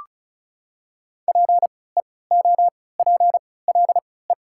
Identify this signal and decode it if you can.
Morse code, heard as a keyed tone, about 33 wpm, spaced as for 20 wpm, 700 Hz. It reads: PEOPLE
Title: 3x Character Spacing - Top 100 Words - 35wpm